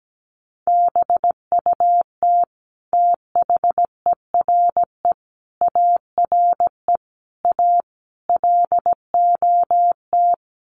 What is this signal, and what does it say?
Morse code, heard as a keyed tone, about 17 wpm, 700 Hz. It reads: BUT THERE ARE A LOT
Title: Sets of 5 Words Encore2 17wpm